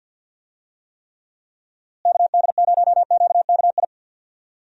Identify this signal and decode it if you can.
Morse code, heard as a keyed tone, about 51 wpm, 700 Hz. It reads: KD0QKI
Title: USA Call Signs - 6 Characters - Repeated 2x - 50wpm